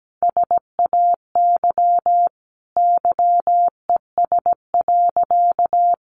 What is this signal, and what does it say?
Morse code, heard as a keyed tone, about 17 wpm, 700 Hz. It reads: SAY YES.
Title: Sentences from Top 500 Words 17wpm